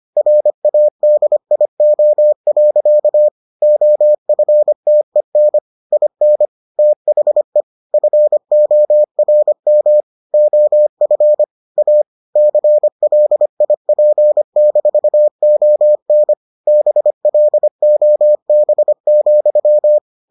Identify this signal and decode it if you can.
Morse code, heard as a keyed tone, about 25 wpm, 600 Hz. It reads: RADIO. OFTEN IN THE FORM OF A CLIP-ON BLOB,